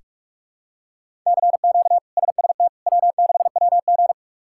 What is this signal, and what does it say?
Morse code, heard as a keyed tone, about 46 wpm, 700 Hz. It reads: CQ SST W6WG
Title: K1USN SST QSO Practice - 45wpm